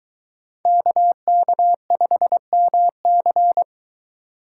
Morse code, heard as a keyed tone, about 22 wpm, 700 Hz. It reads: KK5MC